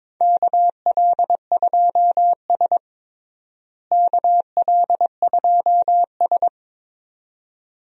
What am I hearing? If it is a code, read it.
Morse code, heard as a keyed tone, about 21 wpm, 700 Hz. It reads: KL2S KL2S
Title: USA Call Signs - 4 Characters - Repeated 2x - 22wpm